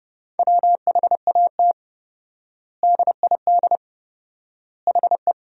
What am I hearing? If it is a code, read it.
Morse code, heard as a keyed tone, about 30 wpm, 700 Hz. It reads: WHAT DID HE
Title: 4x Word-Spacing - Sentences from Top 200 Words - 30wpm